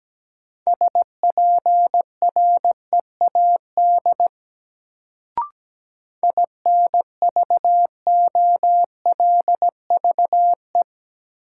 Morse code, heard as a keyed tone, about 17 wpm, 700 Hz. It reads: SPREAD INVOLVE
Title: Sentences from Top 1200 Words - New Word Drill Alt - 17wpm